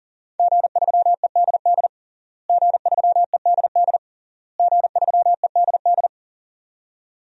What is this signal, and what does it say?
Morse code, heard as a keyed tone, about 40 wpm, 700 Hz. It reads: G3EDD G3EDD G3EDD
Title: DXCC-100 Call Signs - 5 Characters - Repeated 3x - 40wpm